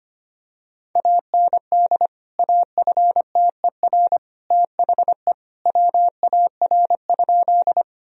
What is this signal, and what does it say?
Morse code, heard as a keyed tone, about 25 wpm, 700 Hz. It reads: AND AFTER THE WAR?
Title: Sentences from Top 1000 Words - Review Part 2 - 25wpm